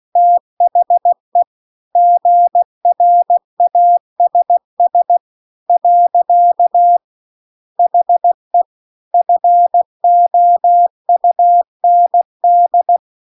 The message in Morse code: THE GRASS . HE FOUND

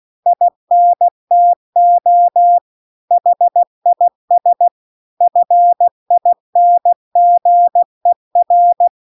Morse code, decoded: INTO HIS FINGER

